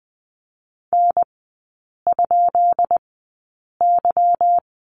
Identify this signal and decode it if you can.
Morse code, heard as a keyed tone, about 20 wpm, spaced as for 8 wpm, 700 Hz. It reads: N?Y
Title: 166 Lesson 27e - Number 7 - Three Characters 20wpm